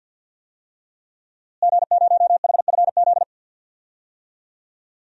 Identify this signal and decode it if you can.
Morse code, heard as a keyed tone, about 51 wpm, 700 Hz. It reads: G0HFZ
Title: DXCC-100 Call Signs - 5 Characters - Repeated 3x - 50wpm